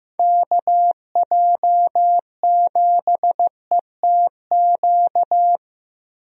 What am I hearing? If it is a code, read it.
Morse code, heard as a keyed tone, about 16 wpm, 700 Hz. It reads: KJ7ETQ
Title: USA Call Signs - 6 Characters - Repeated 2x - 15wpm